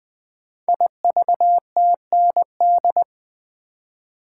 Morse code, heard as a keyed tone, about 20 wpm, 700 Hz. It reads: IVTND
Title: Code Groups - 5 Letters - 20wpm